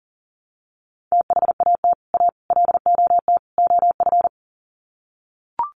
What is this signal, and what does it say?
Morse code, heard as a keyed tone, about 40 wpm, 700 Hz. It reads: THAT A LOT OF
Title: Sets of 4 Words 40wpm